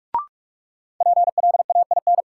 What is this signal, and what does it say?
Morse code, heard as a keyed tone, about 46 wpm, 700 Hz. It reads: PLAIN
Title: Sentences from Top 400 Words - New Word Drill - 45wpm